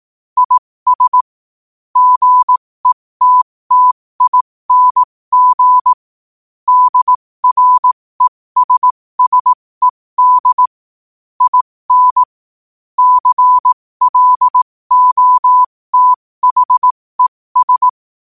Morse code, heard as a keyed tone, about 18 wpm, spaced as for 15 wpm, 1000 Hz. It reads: IS GETTING DRESSED IN CLOTHES
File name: qotdfiles / 15wpm